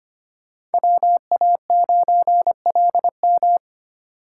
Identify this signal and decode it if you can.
Morse code, heard as a keyed tone, about 33 wpm, 700 Hz. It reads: WA9LM